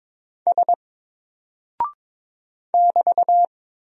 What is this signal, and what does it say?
Morse code, heard as a keyed tone, about 22 wpm, 700 Hz. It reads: S <BT>
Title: Single Character (Letter:Number:Punctuation:Prosign) 22wpm